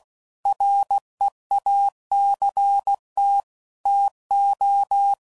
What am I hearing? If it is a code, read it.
Morse code, heard as a keyed tone, about 16 wpm, 800 Hz. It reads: REACT TO
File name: CW Code Groups @15WPM / Technical Subjects / Reactance